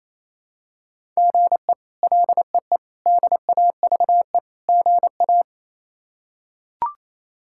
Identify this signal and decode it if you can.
Morse code, heard as a keyed tone, about 28 wpm, 700 Hz. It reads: GE LEE DAVE GA
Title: K1USN SST QSO Practice - 28wpm